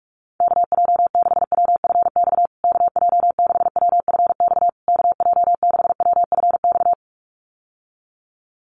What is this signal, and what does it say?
Morse code, heard as a keyed tone, about 44 wpm, 700 Hz. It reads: KJ6WFX KJ6WFX KJ6WFX
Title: USA Call Signs Repeated 3x 45wpm